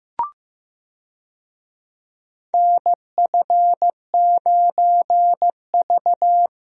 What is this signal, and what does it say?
Morse code, heard as a keyed tone, about 15 wpm, 700 Hz. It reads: NF9V
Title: USA Call Signs - 4 Characters - 15wpm